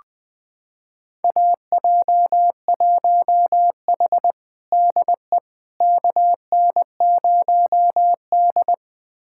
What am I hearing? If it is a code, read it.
Morse code, heard as a keyed tone, about 21 wpm, 700 Hz. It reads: AJ1H DE KN0D
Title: USA Callsign DE Callsign - 4 Characters - 20wpm